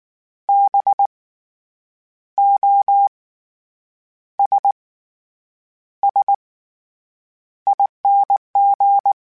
Morse code, heard as a keyed tone, about 19 wpm, spaced as for 8 wpm, 800 Hz. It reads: BOSSING
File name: Special INGs / Special INGs-mp3s / SlowToFaster ing @18